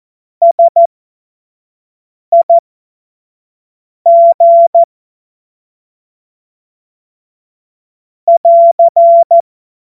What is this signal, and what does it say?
Morse code, heard as a keyed tone, about 14 wpm, spaced as for 5 wpm, 680 Hz. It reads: SIG <AR>
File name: Morsekurs / Witze / Witz 28. September 2021, 25 F70